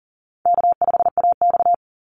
Morse code, heard as a keyed tone, about 40 wpm, 700 Hz. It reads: K5AX
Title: Easy Call Signs Repeated 2x - 40wpm